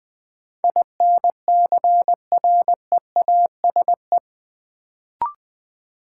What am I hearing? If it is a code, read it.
Morse code, heard as a keyed tone, about 20 wpm, 700 Hz. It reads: INCREASE
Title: Sets of 5 Words - New Word Drill - 20wpm